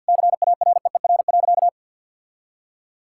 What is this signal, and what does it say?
Morse code, heard as a keyed tone, about 50 wpm, 700 Hz. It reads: CAREER.
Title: Sentences from Top 1000 Words - Review Part 6 - 50wpm